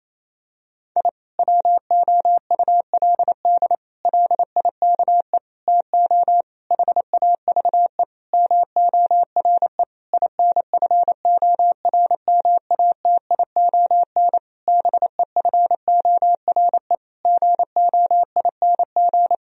Morse code, heard as a keyed tone, about 28 wpm, 700 Hz. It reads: I WOULD LIKE TO HAVE MORE INFORMATION BEFORE GOING
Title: Sentences from Top 1000 Words - Review Part 3 - 28wpm